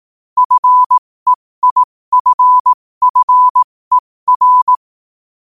Morse code, heard as a keyed tone, about 18 wpm, spaced as for 15 wpm, 1000 Hz. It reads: FEIFFER
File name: qotdfiles / 15wpm